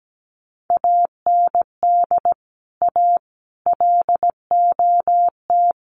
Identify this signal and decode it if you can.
Morse code, heard as a keyed tone, about 17 wpm, 700 Hz. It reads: AND A LOT
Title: Sets of 4 Words 17wpm